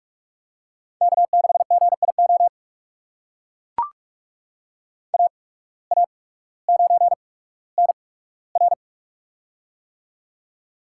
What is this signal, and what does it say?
Morse code, heard as a keyed tone, about 44 wpm, spaced as for 19 wpm, 700 Hz. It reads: K6GIO AA9NR
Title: USA Call Signs - 5 Characters - 8x Character Spacing - 45wpm